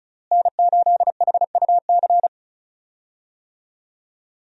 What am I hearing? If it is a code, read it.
Morse code, heard as a keyed tone, about 36 wpm, 700 Hz. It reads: N8HUC
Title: USA Call Signs - 5 Characters - Repeated 3x - 35wpm